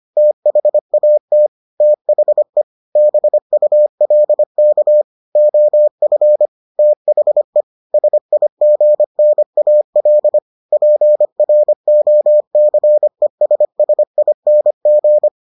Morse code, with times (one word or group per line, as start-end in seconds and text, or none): 0.2-1.5 THAT
1.8-2.6 THE
3.0-5.0 BULK
5.4-6.5 OF
6.8-7.6 THE
7.9-10.4 SIGNAL
10.7-15.3 PROCESSING